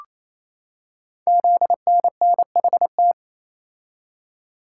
Morse code, heard as a keyed tone, about 27 wpm, 700 Hz. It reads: ZNNHT
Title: Code Groups - 5 Letters - 28wpm